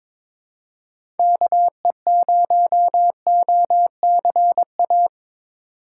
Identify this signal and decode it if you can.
Morse code, heard as a keyed tone, about 22 wpm, 700 Hz. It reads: KE0OCA
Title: USA Call Signs - 6 Characters - 4x Character Spacing - 22wpm